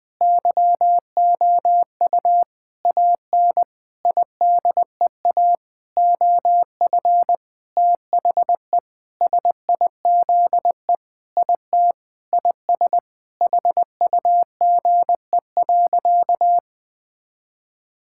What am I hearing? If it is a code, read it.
Morse code, heard as a keyed tone, about 21 wpm, 700 Hz. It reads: YOU AN IDEA OF THE SIZE IT IS HUGE.
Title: Sentences with 6-Word Phrases - 20wpm